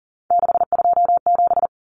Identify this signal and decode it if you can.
Morse code, heard as a keyed tone, about 40 wpm, 700 Hz. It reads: 627
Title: Common Cut Numbers - 3-Digit 40wpm